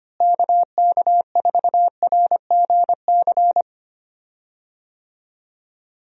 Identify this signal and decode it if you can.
Morse code, heard as a keyed tone, about 25 wpm, 700 Hz. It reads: KK4RGC
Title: USA Call Signs - 6 Characters - Repeated 3x - 25wpm